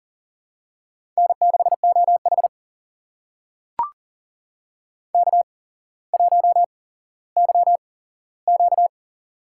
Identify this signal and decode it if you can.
Morse code, heard as a keyed tone, about 40 wpm, spaced as for 17 wpm, 700 Hz. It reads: N6OH K1YQ
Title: USA Call Signs - 4 Characters - 8x Character Spacing - 40wpm